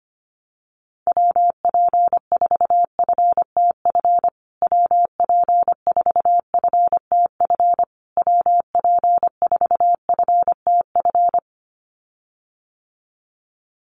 Morse code, heard as a keyed tone, about 25 wpm, 700 Hz. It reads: WP4FTF WP4FTF WP4FTF
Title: USA Call Signs Repeated 3x 25wpm